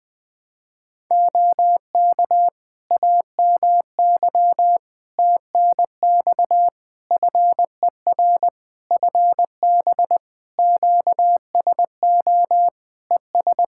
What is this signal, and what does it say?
Morse code, heard as a keyed tone, about 20 wpm, 700 Hz. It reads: OK AMY TNX FER FB QSO ES
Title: Practice QSO - Exchange 4 - Ending - 20wpm